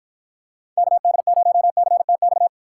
Morse code, heard as a keyed tone, about 52 wpm, 700 Hz. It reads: KD0CTX